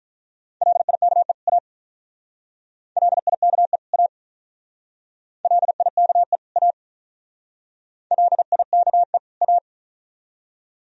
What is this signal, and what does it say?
Morse code, heard as a keyed tone, about 45 wpm, 700 Hz. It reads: LIKE A LIKE A LIKE A LIKE A